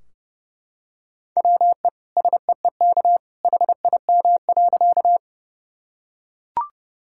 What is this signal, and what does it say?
Morse code, heard as a keyed tone, about 30 wpm, 700 Hz. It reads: WE SEEK HIM.
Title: Sentences from Top 1600 Words 30wpm